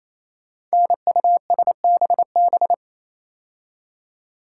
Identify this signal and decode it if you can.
Morse code, heard as a keyed tone, about 29 wpm, 700 Hz. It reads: NUSBB
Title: Code Groups - 5 Letters - 28wpm